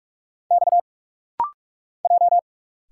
Morse code, heard as a keyed tone, about 46 wpm, 700 Hz. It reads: X J
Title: Single Letters - Rapid-Fire 45wpm